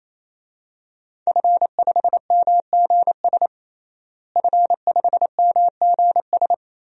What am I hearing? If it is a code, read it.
Morse code, heard as a keyed tone, about 28 wpm, 700 Hz. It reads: F5MGS F5MGS
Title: DXCC-100 Call Signs - 5 Characters - Repeated 3x - 28wpm